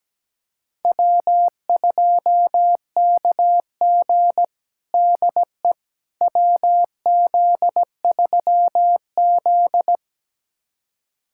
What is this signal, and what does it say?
Morse code, heard as a keyed tone, about 17 wpm, 700 Hz. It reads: W2KG DE WZ3Z
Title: USA Callsign DE Callsign - 4 Characters - 17wpm